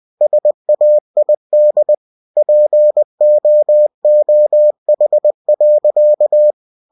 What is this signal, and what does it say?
Morse code, heard as a keyed tone, about 20 wpm, 600 Hz. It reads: SAID POOH.
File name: cw-books / winnie-the-pooh / winnie-20wpm-page-0058